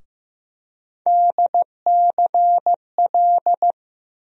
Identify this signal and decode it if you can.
Morse code, heard as a keyed tone, about 15 wpm, 700 Hz. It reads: DCL
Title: USA Call Sign Postfixes - 3 Letters 15wpm